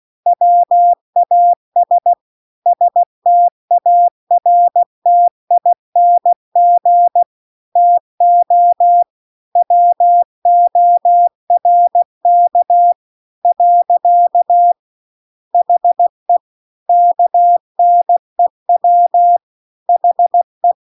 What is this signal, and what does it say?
Morse code, heard as a keyed tone, about 16 wpm, 700 Hz. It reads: WAS STARTING TO WORK . HE KNEW HE